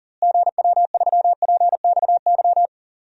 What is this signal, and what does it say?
Morse code, heard as a keyed tone, about 40 wpm, 700 Hz. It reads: GW3PXY
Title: DXCC-100 Call Signs - 6 Characters - Repeated 3x - 40wpm